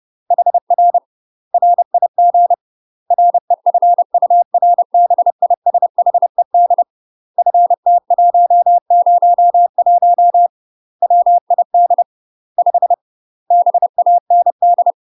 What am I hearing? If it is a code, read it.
Morse code, heard as a keyed tone, about 30 wpm, 700 Hz. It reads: HR RIG REFURBISHED FT101 WID 5 BAND
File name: UK MASTER QSOs TRAINING / 30WPM / UK MASTER QSOs 0042